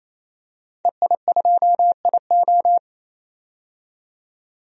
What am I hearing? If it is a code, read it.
Morse code, heard as a keyed tone, about 28 wpm, 700 Hz. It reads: EI2IO
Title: DXCC-100 Call Signs - 5 Characters - 28wpm